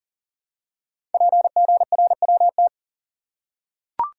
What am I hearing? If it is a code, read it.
Morse code, heard as a keyed tone, about 40 wpm, 700 Hz. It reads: PGRWT
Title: Code Groups - 5 Letters - 40wpm